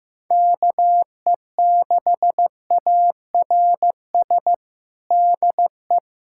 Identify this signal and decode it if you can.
Morse code, heard as a keyed tone, about 15 wpm, 700 Hz. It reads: KE6ARS DE
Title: USA Callsign DE Callsign - 6 Characters - 15wpm